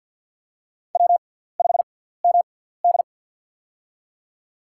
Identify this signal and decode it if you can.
Morse code, heard as a keyed tone, about 49 wpm, spaced as for 17 wpm, 700 Hz. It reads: W5MD
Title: USA Call Signs - 4 Characters - 6x Character Spacing - 50wpm